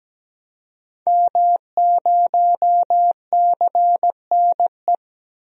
Morse code, heard as a keyed tone, about 17 wpm, 700 Hz. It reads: M0CNE
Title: DXCC-100 Call Signs - 5 Characters - Repeated 3x - 17wpm